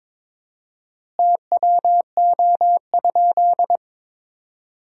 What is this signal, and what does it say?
Morse code, heard as a keyed tone, about 22 wpm, 700 Hz. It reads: TWO?
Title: 4x Word-Spacing - Sentences from Top 100 Words - 22wpm